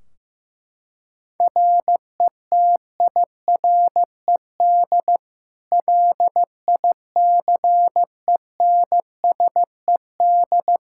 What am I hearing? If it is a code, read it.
Morse code, heard as a keyed tone, about 15 wpm, 700 Hz. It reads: RETIRED LICENSED
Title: Practice QSO - Exchange 3 - More Info - 15wpm